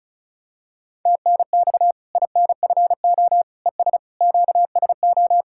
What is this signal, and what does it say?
Morse code, heard as a keyed tone, about 34 wpm, 700 Hz. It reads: TNX INFO ES QSO
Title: Practice QSO - Exchange 4 - Ending - 35wpm